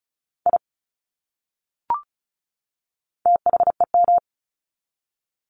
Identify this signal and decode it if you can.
Morse code, heard as a keyed tone, about 34 wpm, 700 Hz. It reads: I THEM
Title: Top 200 Words 35wpm